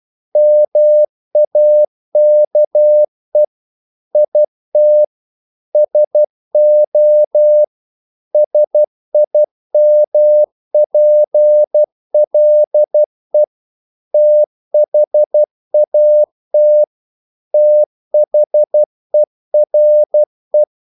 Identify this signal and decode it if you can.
Morse code, heard as a keyed tone, about 12 wpm, 600 Hz. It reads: MAKE IT SO SIMPLE THAT THERE
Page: English quotations in CW (telegraphy - morse)